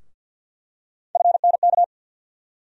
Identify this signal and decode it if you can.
Morse code, heard as a keyed tone, about 48 wpm, 700 Hz. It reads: FNK
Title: USA Call Sign Postfixes - 3 Letters 50wpm